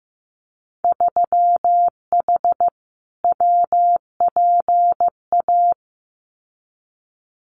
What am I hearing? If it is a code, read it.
Morse code, heard as a keyed tone, about 15 wpm, 700 Hz. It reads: 3H WPA
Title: Winter Field Day Exchanges 15wpm